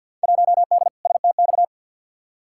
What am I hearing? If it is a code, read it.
Morse code, heard as a keyed tone, about 53 wpm, 700 Hz. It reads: 1D STX
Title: Field Day Exchanges - 50wpm